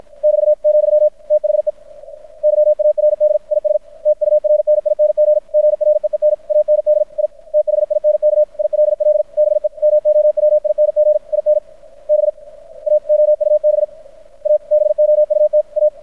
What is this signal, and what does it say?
Morse code, heard as a keyed tone, about 53 wpm, spaced as for 45 wpm, 600 Hz. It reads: <BT> , THE YARD IS TURNING GREEN AND I THINK ILL BE MOWING IN BA OUBABOUT A
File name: archived / CRSnet